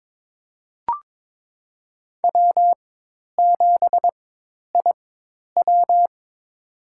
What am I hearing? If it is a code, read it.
Morse code, heard as a keyed tone, about 21 wpm, spaced as for 10 wpm, 700 Hz. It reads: W7IW
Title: USA Call Signs - 4 Characters - 4x Character Spacing - 22wpm